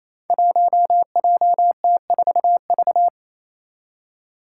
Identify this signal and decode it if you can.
Morse code, heard as a keyed tone, about 27 wpm, 700 Hz. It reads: 1JT4V
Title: Five Challenging Letter-Number Combinations 28wpm